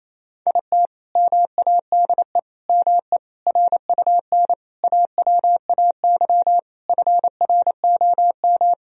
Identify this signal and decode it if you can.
Morse code, heard as a keyed tone, about 28 wpm, 700 Hz. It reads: IT MADE ME RUN AWAY FROM
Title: Sentences from Top 1000 Words - Review Part 1 - 28wpm